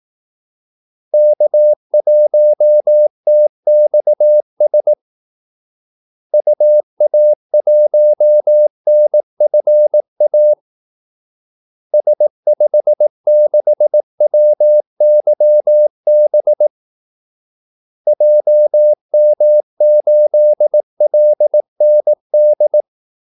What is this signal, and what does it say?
Morse code, heard as a keyed tone, about 18 wpm, 600 Hz. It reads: K1TXS UA1NFA S56WYB JM8LND
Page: Kaldesignaler 18 wpm